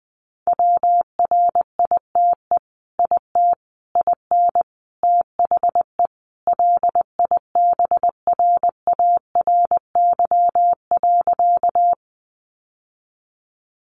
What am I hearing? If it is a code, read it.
Morse code, heard as a keyed tone, about 20 wpm, 700 Hz. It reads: WRITE IT IN THE LIBRARY.
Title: Sentences from Top 600 Words 20wpm